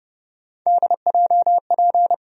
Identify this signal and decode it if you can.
Morse code, heard as a keyed tone, about 30 wpm, 700 Hz. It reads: DJP
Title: USA Call Sign Postfixes - 3 Letters 30wpm